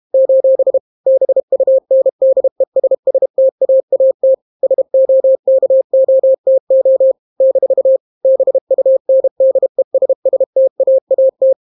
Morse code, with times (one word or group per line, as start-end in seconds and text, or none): 0.1-0.8 :
1.1-4.3 BUNDESSTAAT
4.6-7.1 SOKOTO
7.4-8.0 -
8.2-11.5 BUNDESSTAAT